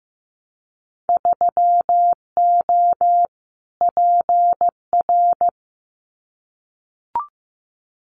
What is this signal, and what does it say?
Morse code, heard as a keyed tone, about 15 wpm, 700 Hz. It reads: 3O PR
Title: Winter Field Day Exchanges Encore 15wpm